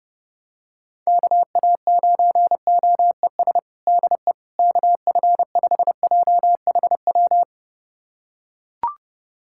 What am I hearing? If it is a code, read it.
Morse code, heard as a keyed tone, about 30 wpm, 700 Hz. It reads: KA9OES DE KF5JHW
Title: USA Callsign DE Callsign 30wpm